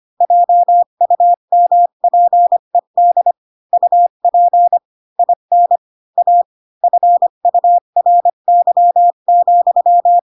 Morse code, heard as a keyed tone, about 25 wpm, 700 Hz. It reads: JUMPED UP IN A FURY,
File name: Pinocchio / Pinocchio ch4 20 25 ik1hge